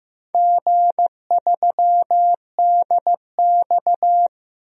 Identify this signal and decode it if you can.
Morse code, heard as a keyed tone, about 16 wpm, 700 Hz. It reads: G3DX
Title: DXCC-100 Call Signs - 4 Characters - Repeated 2x - 15wpm